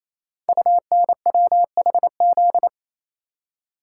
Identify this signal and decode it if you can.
Morse code, heard as a keyed tone, about 28 wpm, 700 Hz. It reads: UNWHZ
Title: Code Groups - 5 Letters - 28wpm